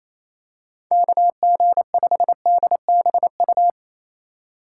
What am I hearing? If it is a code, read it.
Morse code, heard as a keyed tone, about 28 wpm, 700 Hz. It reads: KG5DBU
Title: USA Call Signs - 6 Characters - 4x Character Spacing - 28wpm